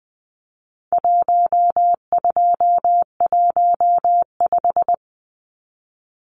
Morse code, heard as a keyed tone, about 20 wpm, 700 Hz. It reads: 1215
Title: Numbers - 4-Digit 20wpm